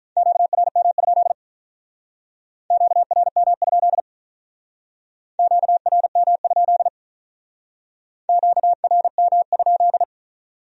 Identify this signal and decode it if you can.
Morse code, heard as a keyed tone, about 46 wpm, 700 Hz. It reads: QRM? QRM? QRM? QRM?